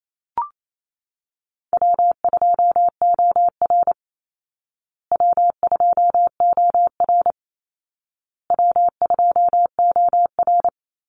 Morse code, heard as a keyed tone, about 28 wpm, 700 Hz. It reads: W2OR W2OR W2OR
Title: Easy Call Signs Repeated 3x - 28wpm